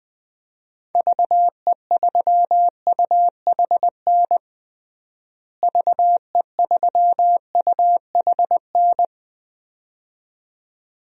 Morse code, heard as a keyed tone, about 21 wpm, 700 Hz. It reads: VE3UHN VE3UHN
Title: DXCC-100 Call Signs - 6 Characters - Repeated 2x - 20wpm